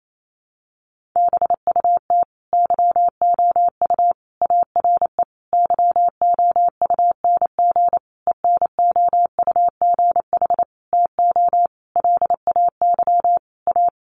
Morse code, heard as a keyed tone, about 28 wpm, 700 Hz. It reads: BUT YOU ARE YOUNG ENOUGH TO LAY A
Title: Sentences from Top 1200 Words 28wpm